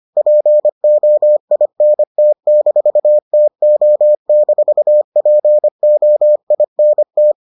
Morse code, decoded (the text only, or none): POINT-TO-POINT